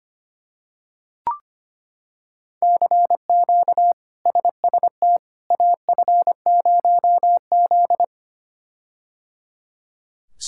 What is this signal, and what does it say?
Morse code, heard as a keyed tone, about 25 wpm, 700 Hz. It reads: CQ SST AF0Z
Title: K1USN SST QSO Practice - 25wpm